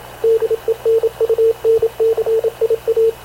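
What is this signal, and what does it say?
Morse code, heard as a keyed tone, about 27 wpm, 450 Hz, with radio noise. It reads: DENUNCIA
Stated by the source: Havana Radio / CLA sending Spanish press (PX) in Morse, recorded by Radio Officer Paul Dumesnil onboard CNAV QUEST/CZDO.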